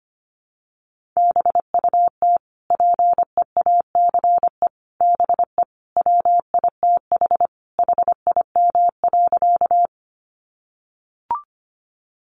Morse code, heard as a keyed tone, about 25 wpm, 700 Hz. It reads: BUT PEACE BE WITH HIM.
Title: Sentences from Top 1500 Words 25wpm